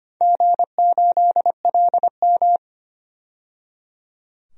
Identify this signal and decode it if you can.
Morse code, heard as a keyed tone, about 25 wpm, 700 Hz. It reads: G8LM
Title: DXCC-100 Call Signs - 4 Characters - 25wpm